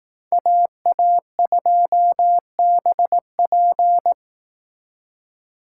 Morse code, heard as a keyed tone, about 18 wpm, 700 Hz. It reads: AA2BP
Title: Speed Racing - USA Call Signs 15wpm